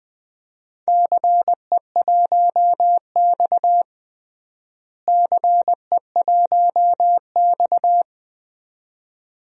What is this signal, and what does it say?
Morse code, heard as a keyed tone, about 20 wpm, 700 Hz. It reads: CE1X CE1X
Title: DXCC-100 Call Signs - 4 Characters - Repeated 2x - 20wpm